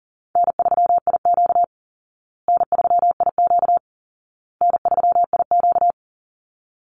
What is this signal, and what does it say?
Morse code, heard as a keyed tone, about 40 wpm, 700 Hz. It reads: N3IQ N3IQ N3IQ
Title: Easy Call Signs Repeated 3x - 40wpm